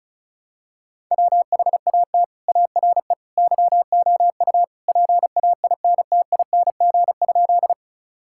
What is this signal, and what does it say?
Morse code, heard as a keyed tone, about 36 wpm, 700 Hz. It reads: WHAT ARE YOU PAINTING?
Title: Sentences from Top 1000 Words - Review Part 6 - 35wpm